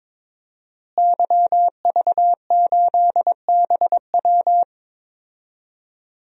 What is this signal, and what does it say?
Morse code, heard as a keyed tone, about 22 wpm, 700 Hz. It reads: YV8BW
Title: DXCC-100 Call Signs - 5 Characters - Repeated 2x - 22wpm